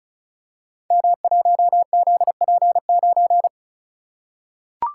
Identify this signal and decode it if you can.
Morse code, heard as a keyed tone, about 36 wpm, 700 Hz. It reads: M1ZP9
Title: Code Groups - 5 Letter:Numbers - 35wpm